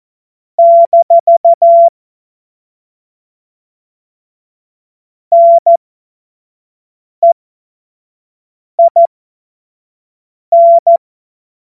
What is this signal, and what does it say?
Morse code, heard as a keyed tone, about 14 wpm, spaced as for 5 wpm, 680 Hz. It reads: - NEIN
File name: Witz-26-August-2025-25-F70